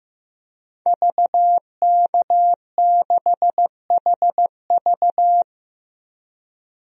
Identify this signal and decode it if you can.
Morse code, heard as a keyed tone, about 15 wpm, 700 Hz. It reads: VK6HV
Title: DXCC-100 Call Signs - 5 Characters - 15wpm